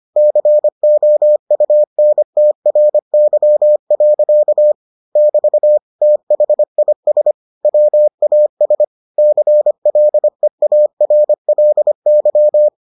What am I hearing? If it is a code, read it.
Morse code, heard as a keyed tone, about 25 wpm, 600 Hz. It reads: COUNTRY. <BT> THIS WAS CLEARLY